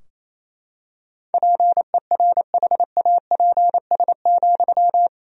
Morse code, heard as a keyed tone, about 28 wpm, 700 Hz. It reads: PERHAPS,
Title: 2x Word-Spacing - Sentences from Top 500 Words - 28wpm